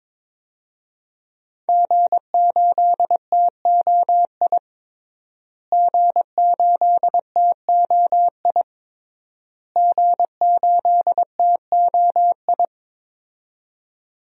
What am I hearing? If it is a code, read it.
Morse code, heard as a keyed tone, about 22 wpm, 700 Hz. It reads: G8TOI G8TOI G8TOI
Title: DXCC-100 Call Signs - 5 Characters - Repeated 3x - 22wpm